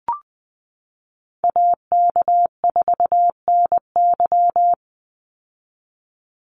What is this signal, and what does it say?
Morse code, heard as a keyed tone, about 20 wpm, 700 Hz. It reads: AK4NY
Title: Call Signs Repeated 2x - 20wpm